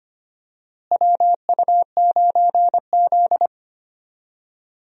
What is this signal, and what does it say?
Morse code, heard as a keyed tone, about 25 wpm, 700 Hz. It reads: WU9Z
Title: USA Call Signs - 4 Characters - Repeated 3x - 25wpm